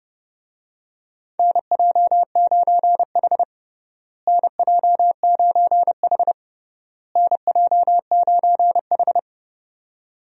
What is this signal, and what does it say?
Morse code, heard as a keyed tone, about 30 wpm, 700 Hz. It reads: NJ9H NJ9H NJ9H
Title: USA Call Signs - 4 Characters - Repeated 3x - 30wpm